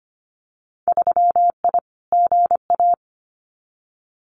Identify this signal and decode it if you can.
Morse code, heard as a keyed tone, about 25 wpm, 700 Hz. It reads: 3I GA
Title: Winter Field Day Exchanges 25wpm